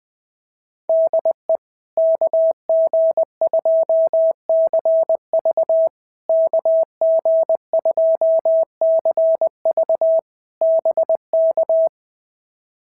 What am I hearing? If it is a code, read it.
Morse code, heard as a keyed tone, about 19 wpm, 650 Hz. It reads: DE KG2CV KG2CV BK
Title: POTA QSO Practice - 20wpm